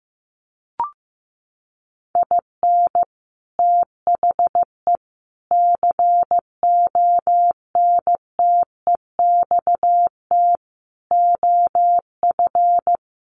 Morse code, heard as a keyed tone, about 15 wpm, 700 Hz. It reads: IN THE CONTEXT OF
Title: Sets of 5 Words 15wpm